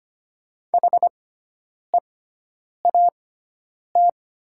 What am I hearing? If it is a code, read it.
Morse code, heard as a keyed tone, about 25 wpm, spaced as for 9 wpm, 700 Hz. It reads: HEAT
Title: 6x Character Spacing - Top 500 Words - 25wpm